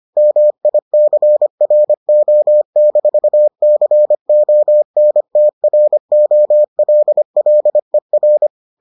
Morse code, heard as a keyed tone, about 25 wpm, 600 Hz. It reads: MICRO-CONTROLLER